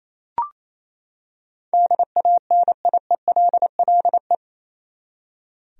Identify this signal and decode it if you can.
Morse code, heard as a keyed tone, about 28 wpm, 700 Hz. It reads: DANIELLE
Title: USA Common Names - 28wpm